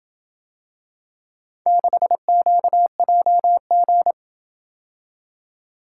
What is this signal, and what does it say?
Morse code, heard as a keyed tone, about 27 wpm, 700 Hz. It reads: 6QJG